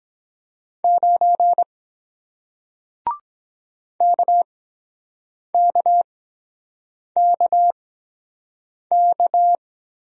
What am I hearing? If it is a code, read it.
Morse code, heard as a keyed tone, about 23 wpm, spaced as for 7 wpm, 700 Hz. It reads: 9 KKKK